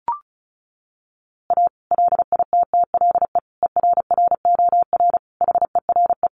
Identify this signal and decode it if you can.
Morse code, heard as a keyed tone, about 34 wpm, 700 Hz. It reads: A LITTLE ERROR HERE
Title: A Sound Of Thunder 35wpm